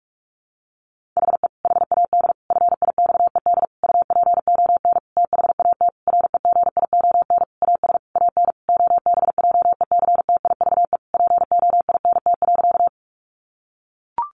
Morse code, heard as a keyed tone, about 44 wpm, 700 Hz. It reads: HE HAD FIXED UPON THAT REGION AS AN OBJECTIVE POINT.
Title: Sentences from Top 1500 Words 45wpm